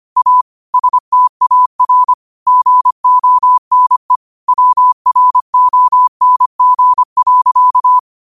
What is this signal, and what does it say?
Morse code, heard as a keyed tone, about 25 wpm, 1000 Hz. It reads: A STAR GONE WRONG.
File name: qotdfiles / 25wpm